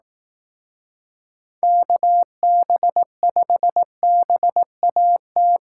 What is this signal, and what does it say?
Morse code, heard as a keyed tone, about 18 wpm, 700 Hz. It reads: KB5BAT
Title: Speed Racing - USA Call Signs 15wpm